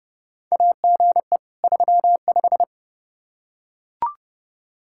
Morse code, heard as a keyed tone, about 31 wpm, 700 Hz. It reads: AGE 35
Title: Practice QSO - Exchange 3 - More Info - 30wpm